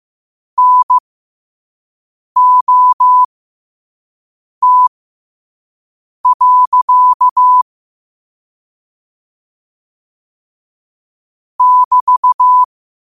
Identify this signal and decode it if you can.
Morse code, heard as a keyed tone, about 15 wpm, spaced as for 5 wpm, 1000 Hz. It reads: NOT. <BT>
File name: qotdfiles / 5wpm